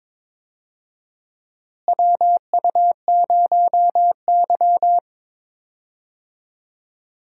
Morse code, heard as a keyed tone, about 21 wpm, 700 Hz. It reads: WU0Y
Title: USA Call Signs - 4 Characters - Repeated 2x - 22wpm